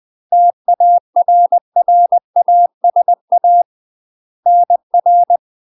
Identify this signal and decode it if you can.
Morse code, heard as a keyed tone, about 20 wpm, 700 Hz. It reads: TARRASA NR